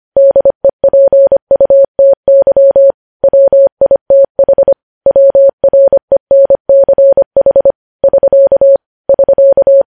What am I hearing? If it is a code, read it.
Morse code, heard as a keyed tone, about 25 wpm, 550 Hz. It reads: DEPUTY WITH WRENCH <SK> <SK>
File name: podcast / News headlines-1x-25x25